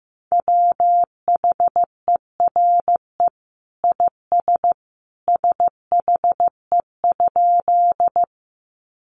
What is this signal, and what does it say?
Morse code, heard as a keyed tone, about 15 wpm, 700 Hz. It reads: WHERE IS SHE?
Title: Sentences from Top 200 Words 15wpm